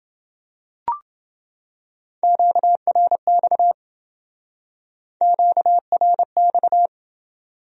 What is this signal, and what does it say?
Morse code, heard as a keyed tone, about 28 wpm, 700 Hz. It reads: QRX QRX